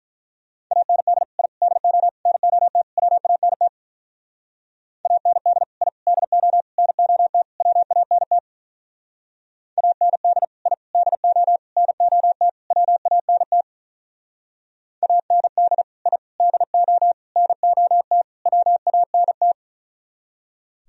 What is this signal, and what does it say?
Morse code, heard as a keyed tone, about 44 wpm, 700 Hz. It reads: AND I DO NOT WANT AND I DO NOT WANT AND I DO NOT WANT AND I DO NOT WANT
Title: Speed-Racing - Sets of 5 Words 35wpm